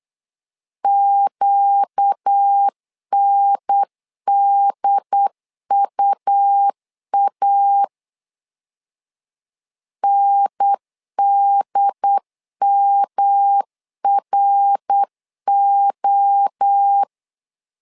Morse code, heard as a keyed tone, about 8 wpm, 800 Hz. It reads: QNDUA NDMRO
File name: 4 Q 8wpm